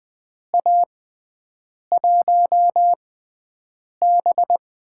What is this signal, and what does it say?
Morse code, heard as a keyed tone, about 20 wpm, spaced as for 7 wpm, 700 Hz. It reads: A1B
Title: USA Call Signs - 4 Characters - 6x Character Spacing - 20wpm